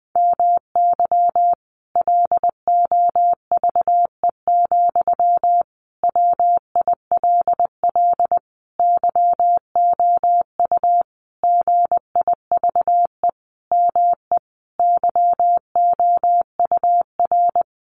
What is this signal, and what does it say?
Morse code, heard as a keyed tone, about 20 wpm, 700 Hz. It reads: MY LOVE, WILL YOU GIVE ME YOUR
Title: Sentences from Top 1200 Words 20wpm